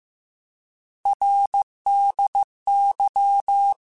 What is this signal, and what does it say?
Morse code, heard as a keyed tone, about 15 wpm, 800 Hz. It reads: RDY
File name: Abreviations / Abreviations mp3s / Abreviations@13